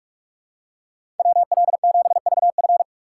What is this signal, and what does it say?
Morse code, heard as a keyed tone, about 42 wpm, 700 Hz. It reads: WL7VF
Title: USA Call Signs - 5 Characters - 2x Character Spacing - 45wpm